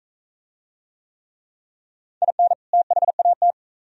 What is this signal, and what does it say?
Morse code, heard as a keyed tone, about 43 wpm, 700 Hz. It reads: IN THAT